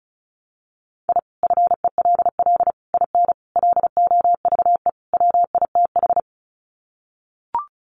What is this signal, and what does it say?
Morse code, heard as a keyed tone, about 34 wpm, 700 Hz. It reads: I FELL IN LOVE WITH
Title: Sets of 5 Words Encore 35wpm